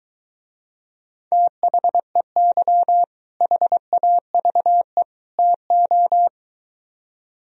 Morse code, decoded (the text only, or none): THEY HAVE TO